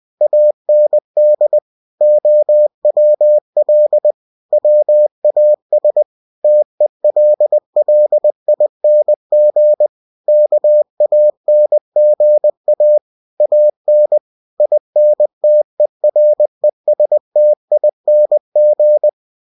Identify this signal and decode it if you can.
Morse code, heard as a keyed tone, about 20 wpm, 600 Hz. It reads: AND OWL WAS TELLING KANGA AN INTERESTING
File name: cw-books / winnie-the-pooh / winnie-20wpm-page-0085